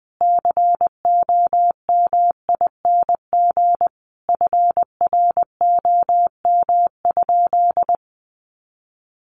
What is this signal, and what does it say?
Morse code, heard as a keyed tone, about 20 wpm, 700 Hz. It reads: COMING FROM?
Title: Sentences from Top 1400 Words 20wpm